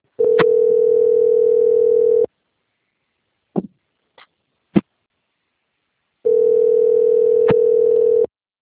Ringback tone (North America), 440 + 480 Hz.